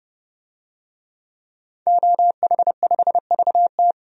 Morse code, heard as a keyed tone, about 30 wpm, 700 Hz. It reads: OH5VT
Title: DXCC-100 Call Signs - 5 Characters - 30wpm